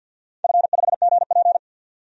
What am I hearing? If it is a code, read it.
Morse code, heard as a keyed tone, about 52 wpm, 700 Hz. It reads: F5GP
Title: DXCC-100 Call Signs - 4 Characters - 50wpm